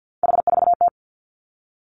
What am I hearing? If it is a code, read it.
Morse code, heard as a keyed tone, about 50 wpm, 700 Hz. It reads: H4T
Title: Three Challenging Letter-Number Combinations 50wpm